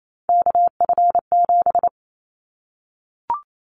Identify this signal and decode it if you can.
Morse code, heard as a keyed tone, about 28 wpm, 700 Hz. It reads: KF7
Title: Callsign Prefixes for Top 25 DXCC - 28wpm